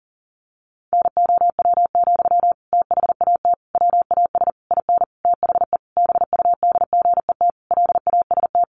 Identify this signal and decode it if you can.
Morse code, heard as a keyed tone, about 38 wpm, 700 Hz. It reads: NOW, THAT WAS IN THE BUDGET LAST
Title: Sentences from Top 1200 Words 40wpm